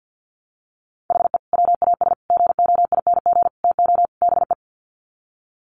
Morse code, heard as a keyed tone, about 50 wpm, 700 Hz. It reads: HE WAS GOING TO BE
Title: Sets of 5 Words 50wpm